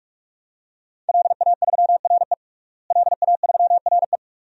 Morse code, heard as a keyed tone, about 44 wpm, 700 Hz. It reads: LA3RE LA3RE
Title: DXCC-100 Call Signs - 5 Characters - Repeated 2x - 45wpm